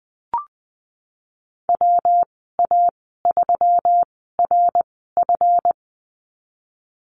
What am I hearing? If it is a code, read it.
Morse code, heard as a keyed tone, about 20 wpm, spaced as for 15 wpm, 700 Hz. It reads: WA3RF
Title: 154 Lesson 25g - Letter p - Call Signs 20wpm